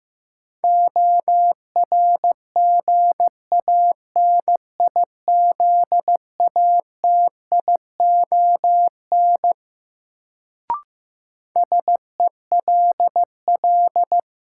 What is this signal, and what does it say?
Morse code, heard as a keyed tone, about 15 wpm, 700 Hz. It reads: ORGANIZATION SELL
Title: Sentences from Top 1100 Words - New Word Drill Alt - 15wpm